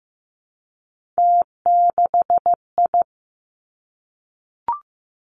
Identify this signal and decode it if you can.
Morse code, heard as a keyed tone, about 15 wpm, 700 Hz. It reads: T6I
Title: Three Challenging Letter-Number Combinations 15wpm